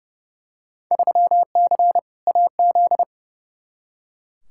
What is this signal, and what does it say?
Morse code, heard as a keyed tone, about 30 wpm, 700 Hz. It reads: 3C AZ
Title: Field Day Exchanges Encore - 30wpm